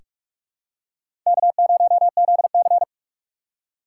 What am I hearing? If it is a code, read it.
Morse code, heard as a keyed tone, about 47 wpm, 700 Hz. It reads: K0ZC
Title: USA Call Signs - 3x Character Spacing - 45wpm